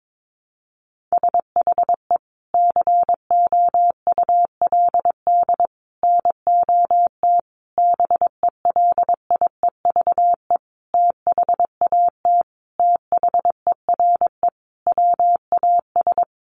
Morse code, heard as a keyed tone, about 22 wpm, 700 Hz. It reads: SHE COULD NOT BELIEVE THAT THERE WAS
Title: Sentences from Top 1000 Words 22wpm